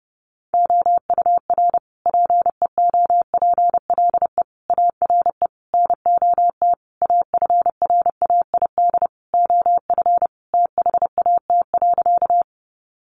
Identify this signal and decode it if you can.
Morse code, heard as a keyed tone, about 30 wpm, 700 Hz. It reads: OUR PEOPLE ARE NOT AFRAID OF THAT.
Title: Sentences from Top 1500 Words 30wpm